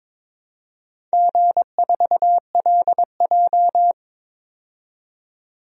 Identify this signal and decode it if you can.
Morse code, heard as a keyed tone, about 22 wpm, 700 Hz. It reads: G4LJ
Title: DXCC-100 Call Signs - 4 Characters - 22wpm